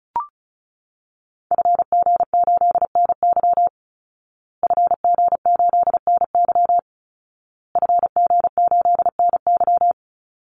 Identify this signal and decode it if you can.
Morse code, heard as a keyed tone, about 34 wpm, 700 Hz. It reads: FG8NY FG8NY FG8NY
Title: Call Signs Repeated 2x - 35wpm